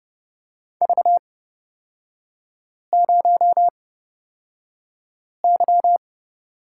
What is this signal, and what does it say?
Morse code, heard as a keyed tone, about 30 wpm, spaced as for 5 wpm, 690 Hz. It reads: V0Y
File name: course / 30:10wpm / 236 Lesson 38e - Letter q - Three Characters 30wpm